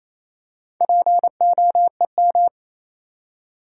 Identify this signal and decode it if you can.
Morse code, heard as a keyed tone, about 28 wpm, 700 Hz. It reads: POEM
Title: Four Letter Words - 28wpm